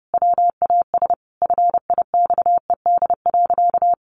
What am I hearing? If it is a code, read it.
Morse code, heard as a keyed tone, about 30 wpm, 700 Hz. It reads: WAS FIXED.
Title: Sentences from Top 1100 Words 30wpm